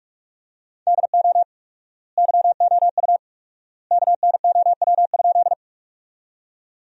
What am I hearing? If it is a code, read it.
Morse code, heard as a keyed tone, about 46 wpm, 700 Hz. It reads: DO YOU KNOW?
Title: 4x Word-Spacing - Sentences from Top 100 Words - 45wpm